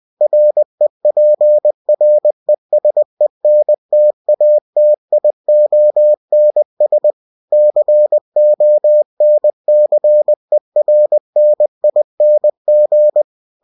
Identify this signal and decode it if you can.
Morse code, heard as a keyed tone, about 20 wpm, 600 Hz. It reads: REPRESENTATIONS CONCERNING
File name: cw-books / winnie-the-pooh / winnie-20wpm-page-0114